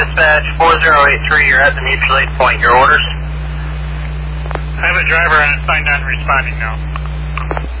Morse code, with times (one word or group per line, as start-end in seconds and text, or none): none